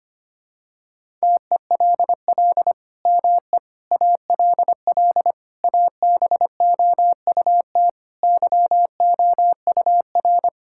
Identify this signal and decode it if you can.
Morse code, heard as a keyed tone, about 25 wpm, 700 Hz. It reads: TELL ME ALL ABOUT YOUR
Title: Sentences from Top 1000 Words - Review Part 5 - 25wpm